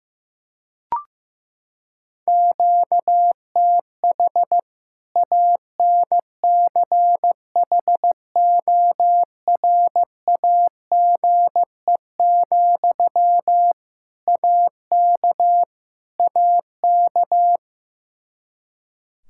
Morse code, heard as a keyed tone, about 16 wpm, 700 Hz. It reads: QTH ANCHORAGE, AK AK
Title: K1USN SST QSO Practice 15wpm